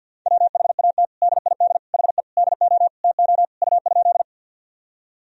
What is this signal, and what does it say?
Morse code, heard as a keyed tone, about 49 wpm, 700 Hz. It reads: WHAT DID HE DO TO U?
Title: Sentences from Top 100 Words and Abbreviations 50wpm